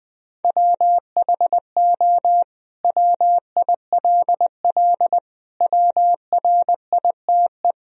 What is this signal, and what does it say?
Morse code, heard as a keyed tone, about 20 wpm, 700 Hz. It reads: WHO WILL WRITE
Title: Sentences from Top 1000 Words - Review Part 3 - 20wpm